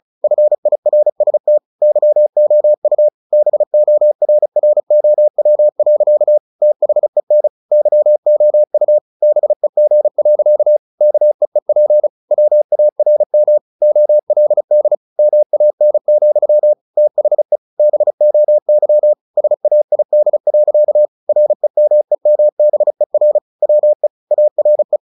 Morse code, heard as a keyed tone, about 34 wpm, 600 Hz. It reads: FIRST YOU BORROW. THEN YOU BEG. KEEP WARM OLD MAN, THE BOY SAID. REMEMBER WE ARE
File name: the old man and the sea / the old man and the sea 1 35wpm